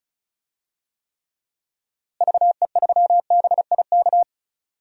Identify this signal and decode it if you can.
Morse code, heard as a keyed tone, about 34 wpm, 700 Hz. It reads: VE3BIK
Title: DXCC-100 Call Signs - 6 Characters - Repeated 3x - 35wpm